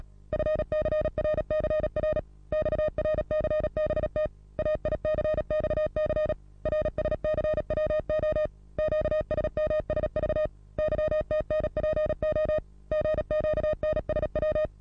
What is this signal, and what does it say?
Morse code, heard as a keyed tone, about 36 wpm, 620 Hz. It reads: FCRCR XRCBT AICXC RSCWO QSMSV YTNPO GQNSW
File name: repos / Morsekurs / DJ4UF Morsen27